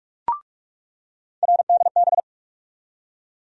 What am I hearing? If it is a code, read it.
Morse code, heard as a keyed tone, about 46 wpm, 700 Hz. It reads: RDB
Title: USA Call Sign Postfixes - 3 Letters 45wpm